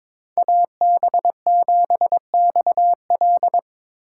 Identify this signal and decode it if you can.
Morse code, heard as a keyed tone, about 25 wpm, 700 Hz. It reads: AB7XL
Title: Speed Racing - USA 2x2 Call Signs 22wpm